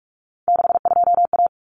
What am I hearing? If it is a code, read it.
Morse code, heard as a keyed tone, about 44 wpm, 700 Hz. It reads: 62A
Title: Common Cut Numbers - 3-Digit 45wpm